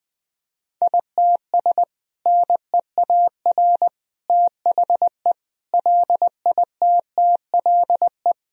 Morse code, heard as a keyed tone, about 20 wpm, 700 Hz. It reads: ITS NEAR THE LITTLE
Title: Sentences from Top 1000 Words - Review Part 2 - 20wpm